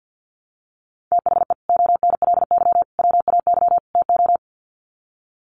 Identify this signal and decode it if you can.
Morse code, heard as a keyed tone, about 50 wpm, 700 Hz. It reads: THE ONLY WAY TO
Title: Sets of 4 Words 50wpm